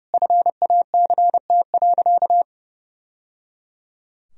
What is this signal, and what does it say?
Morse code, heard as a keyed tone, about 28 wpm, 700 Hz. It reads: FACT.
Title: Sentences from Top 1000 Words - Review Part 4 - 30wpm